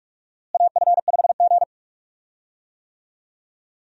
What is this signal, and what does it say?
Morse code, heard as a keyed tone, about 42 wpm, 700 Hz. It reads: AF5G
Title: USA Call Signs - 4 Characters - Repeated 2x - 45wpm